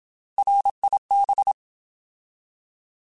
Morse code, heard as a keyed tone, about 27 wpm, 800 Hz. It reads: RIB
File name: Alphabetical / Alphabetical mp3s / Rs @25